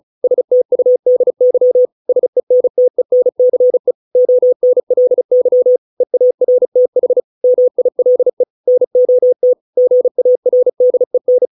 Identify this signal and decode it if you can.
Morse code, heard as a keyed tone, about 34 wpm, 500 Hz. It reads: STUDY SENTENCE ONLY EARTH MILE NOT GARDEN